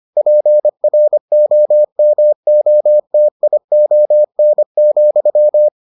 Morse code, heard as a keyed tone, about 25 wpm, 600 Hz. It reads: PROMOTION,